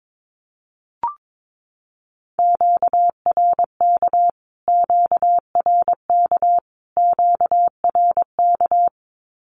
Morse code, heard as a keyed tone, about 22 wpm, 700 Hz. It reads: QRK QRK QRK
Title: Q-Codes Repeated 3x 22wpm